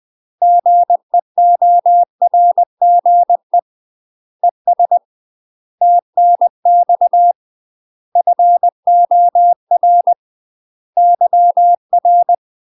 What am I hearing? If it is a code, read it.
Morse code, heard as a keyed tone, about 20 wpm, 700 Hz. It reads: GEORGE ES TNX FOR YR
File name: UK MASTER QSOs TRAINING / 20WPM / UK MASTER QSOs 0031